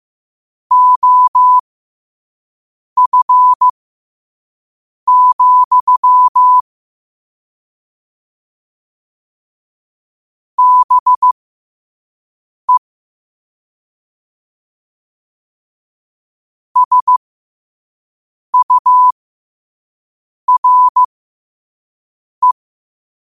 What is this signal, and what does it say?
Morse code, heard as a keyed tone, about 15 wpm, spaced as for 5 wpm, 1000 Hz. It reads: OF, BE SURE
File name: qotdfiles / 5wpm